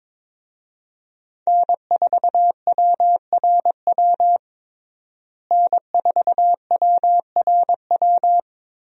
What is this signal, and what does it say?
Morse code, heard as a keyed tone, about 22 wpm, 700 Hz. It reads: N4WRW N4WRW
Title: USA Call Signs - 5 Characters - Repeated 3x - 22wpm